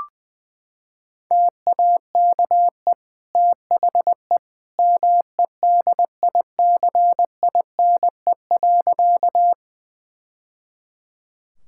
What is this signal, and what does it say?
Morse code, heard as a keyed tone, about 21 wpm, 700 Hz. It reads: TAKE THE MEDICINE.
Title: Sentences from Top 1000 Words - Review Part 6 - 20wpm